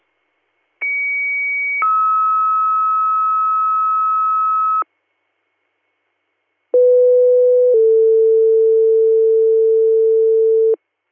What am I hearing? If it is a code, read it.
Two-tone page, heard as a one-second tone, then a three-second tone, about 2250 then 1290 Hz, about 500 then 450 Hz. It is heard from 0.8 to 10.7 s.